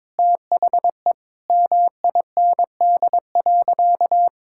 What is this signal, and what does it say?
Morse code, heard as a keyed tone, about 22 wpm, 700 Hz. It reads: THE MIND.
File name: practice / 22wpm / Sentences with 4-Word Phrases - 22wpm